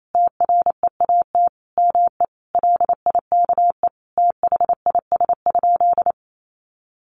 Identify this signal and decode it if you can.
Morse code, heard as a keyed tone, about 28 wpm, 700 Hz. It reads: TREAT ME LIKE THIS?
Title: Sentences from Top 1300 Words 28wpm